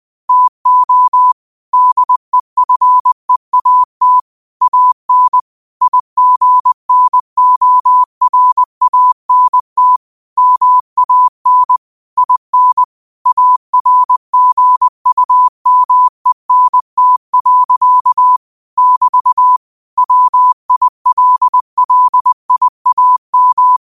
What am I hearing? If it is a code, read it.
Morse code, heard as a keyed tone, about 20 wpm, 1000 Hz. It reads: TO DEFEAT AN IGNORANT MAN IN ARGUMENT. <BT> WILLIAM
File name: qotdfiles / 20wpm